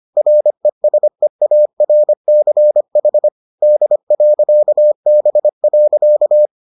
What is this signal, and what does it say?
Morse code, heard as a keyed tone, about 25 wpm, 600 Hz. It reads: RESEARCH D.B.